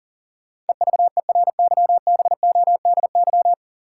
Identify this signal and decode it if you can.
Morse code, heard as a keyed tone, about 40 wpm, 700 Hz. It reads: EVERYBODY
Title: Sentences from Top 1400 Words - New Word Drill Alt - 40wpm